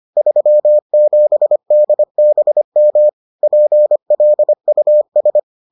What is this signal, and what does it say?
Morse code, heard as a keyed tone, about 25 wpm, 600 Hz. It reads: 37 DBM PLUS